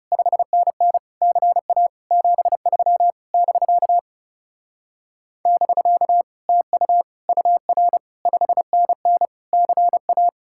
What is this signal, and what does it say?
Morse code, heard as a keyed tone, about 36 wpm, 700 Hz. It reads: 5NN CA 73 <BK> <BK> TU UR 5NN CA
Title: Speed-Racing Common QSO Phrases - 30wpm